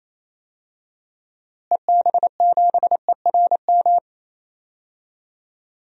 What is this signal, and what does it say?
Morse code, heard as a keyed tone, about 28 wpm, 700 Hz. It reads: EB7ERM
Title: DXCC-100 Call Signs - 6 Characters - Repeated 2x - 28wpm